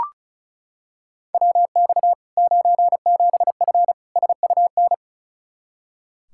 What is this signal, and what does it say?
Morse code, heard as a keyed tone, about 34 wpm, 700 Hz. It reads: WX 97F SUN
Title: Practice QSO - Exchange 2 - Info - 35wpm